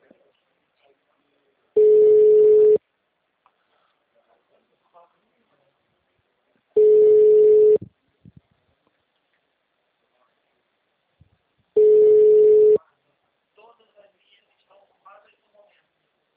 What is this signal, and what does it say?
Ringback tone (Europe, 425 Hz), with speech.
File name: out- -2032- - - .3387